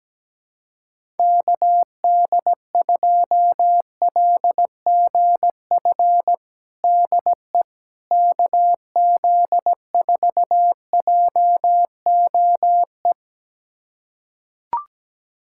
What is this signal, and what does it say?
Morse code, heard as a keyed tone, about 17 wpm, 700 Hz. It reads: KD2LGF DE KZ4JOE
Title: USA Callsign DE Callsign - 6 Characters - 17wpm